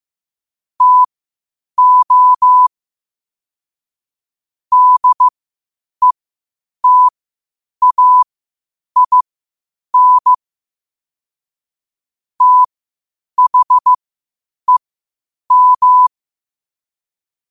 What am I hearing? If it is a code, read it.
Morse code, heard as a keyed tone, about 15 wpm, spaced as for 8 wpm, 1000 Hz. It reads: TO DETAIN THEM
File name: qotdfiles / 8wpm